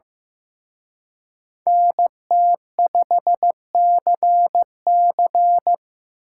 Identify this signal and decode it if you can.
Morse code, heard as a keyed tone, about 18 wpm, 700 Hz. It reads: NT5CC
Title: Speed Racing - USA 2x2 Call Signs 15wpm